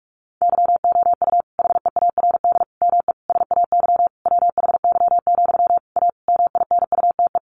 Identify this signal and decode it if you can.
Morse code, heard as a keyed tone, about 44 wpm, 700 Hz. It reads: YOU HEARD ME SAY WHY, A MINUTE
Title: Sentences from Top 500 Words 45wpm